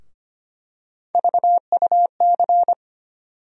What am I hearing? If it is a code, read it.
Morse code, heard as a keyed tone, about 25 wpm, 700 Hz. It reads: VUC
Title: USA Call Sign Postfixes - 3 Letters 25wpm